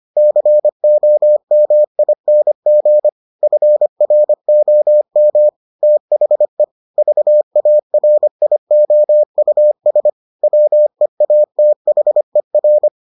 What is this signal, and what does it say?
Morse code, heard as a keyed tone, about 25 wpm, 600 Hz. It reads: COMING FROM THE VARIOUS WEATHER